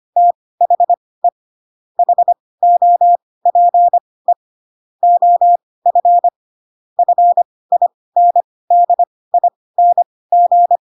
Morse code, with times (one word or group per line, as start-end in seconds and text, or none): 0.2-1.3 THE
2.0-4.3 HOPE
5.0-6.3 OF
7.0-10.8 FINDING